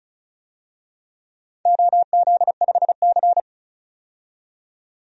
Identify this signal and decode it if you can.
Morse code, heard as a keyed tone, about 36 wpm, 700 Hz. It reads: OZ5C
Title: DXCC-100 Call Signs - 4 Characters - 35wpm